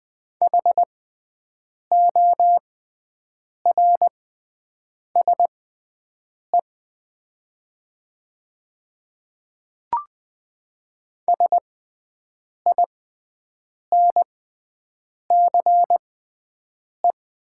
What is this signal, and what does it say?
Morse code, heard as a keyed tone, about 20 wpm, spaced as for 7 wpm, 700 Hz. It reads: HORSE SINCE
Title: 6x Character Spacing - Top 300 Words - 20wpm